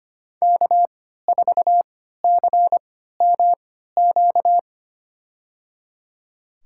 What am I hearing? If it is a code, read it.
Morse code, heard as a keyed tone, about 25 wpm, spaced as for 14 wpm, 700 Hz. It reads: K4CMQ
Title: USA Call Signs - 3x Character Spacing - 25wpm